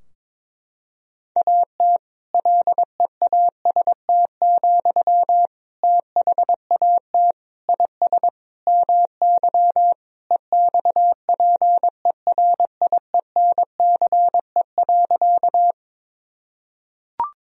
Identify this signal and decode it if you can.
Morse code, heard as a keyed tone, about 22 wpm, 700 Hz. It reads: AT LEAST, THAT IS MY EXPERIENCE.
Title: Sentences from Top 1000 Words - Review Part 3 - 22wpm